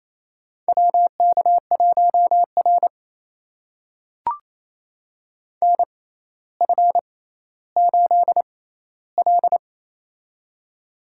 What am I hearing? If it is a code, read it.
Morse code, heard as a keyed tone, about 27 wpm, spaced as for 14 wpm, 700 Hz. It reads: WK1R NF8L
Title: USA Call Signs - 4 Characters - 6x Character Spacing - 28wpm